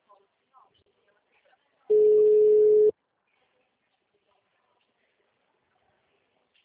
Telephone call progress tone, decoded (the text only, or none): ringback tone